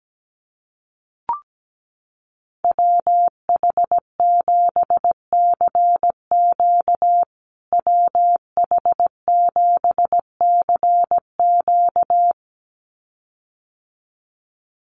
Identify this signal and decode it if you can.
Morse code, heard as a keyed tone, about 17 wpm, 700 Hz. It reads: WH7CQ WH7CQ
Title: USA Call Signs Repeated 2x 17wpm